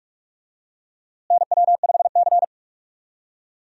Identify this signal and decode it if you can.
Morse code, heard as a keyed tone, about 46 wpm, 700 Hz. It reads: NW5C
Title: USA Call Signs - 4 Characters - Repeated 3x - 45wpm